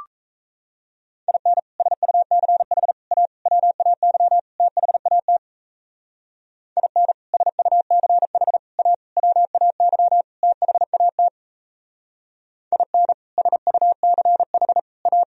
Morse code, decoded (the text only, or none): IN SUCH A WAY THAT IN SUCH A WAY THAT IN SUCH A